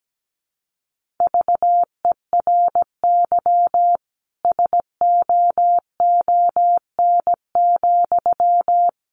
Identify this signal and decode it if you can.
Morse code, heard as a keyed tone, about 17 wpm, 700 Hz. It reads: VERY SOON,
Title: Sentences from Top 1400 Words 17wpm